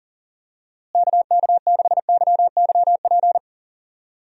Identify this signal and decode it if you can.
Morse code, heard as a keyed tone, about 37 wpm, 700 Hz. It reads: KK6YYP
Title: USA Call Signs - 6 Characters - Repeated 2x - 40wpm